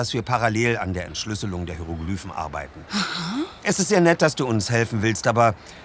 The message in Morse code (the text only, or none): none